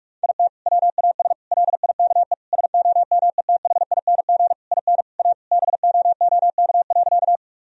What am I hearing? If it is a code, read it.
Morse code, heard as a keyed tone, about 41 wpm, 700 Hz. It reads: IT WAS LIKE SOMETHING IN A BOOK.
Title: Sentences from Top 1000 Words - Review Part 1 - 45wpm